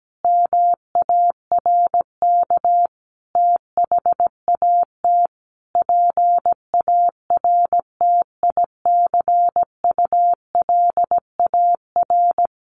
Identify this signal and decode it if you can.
Morse code, heard as a keyed tone, about 17 wpm, 700 Hz. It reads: MARK THAT PARTICULAR
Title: Sentences from Top 1400 Words 17wpm